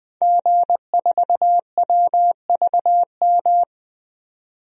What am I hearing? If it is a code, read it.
Morse code, heard as a keyed tone, about 20 wpm, 700 Hz. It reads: G4WVM
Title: DXCC-100 Call Signs - 5 Characters - 20wpm